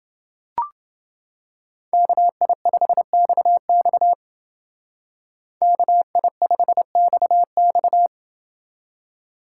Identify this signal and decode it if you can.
Morse code, heard as a keyed tone, about 28 wpm, 700 Hz. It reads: KI5XX KI5XX